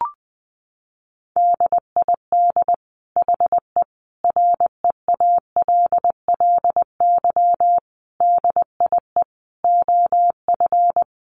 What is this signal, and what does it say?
Morse code, heard as a keyed tone, about 20 wpm, 700 Hz. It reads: DID HE REALLY DIE OF
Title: Sentences from Top 1300 Words 20wpm